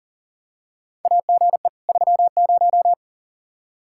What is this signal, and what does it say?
Morse code, heard as a keyed tone, about 38 wpm, 700 Hz. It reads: AGE 30
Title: Practice QSO - Exchange 3 - More Info - 40wpm